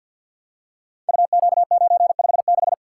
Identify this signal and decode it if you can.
Morse code, heard as a keyed tone, about 53 wpm, 700 Hz. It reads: UQ956
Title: Code Groups - 5 Letter:Numbers - 50wpm